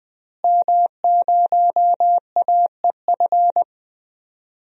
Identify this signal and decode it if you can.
Morse code, heard as a keyed tone, about 20 wpm, 700 Hz. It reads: M0AEF
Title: DXCC-100 Call Signs - 5 Characters - Repeated 2x - 20wpm